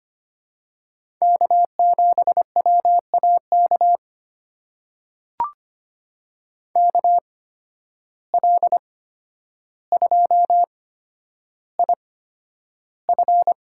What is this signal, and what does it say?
Morse code, heard as a keyed tone, about 25 wpm, spaced as for 11 wpm, 700 Hz. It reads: K7WAK KL2IF
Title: USA Call Signs - 5 Characters - 8x Character Spacing - 25wpm